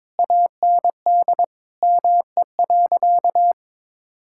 Morse code, heard as a keyed tone, about 22 wpm, 700 Hz. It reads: AND ME.
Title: Sentences from Top 1000 Words - Review Part 6 - 22wpm